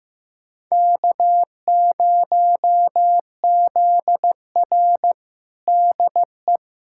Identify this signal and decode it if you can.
Morse code, heard as a keyed tone, about 15 wpm, 700 Hz. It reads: K0ZR DE
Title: Practice QSO - Exchange 4 - Ending - 15wpm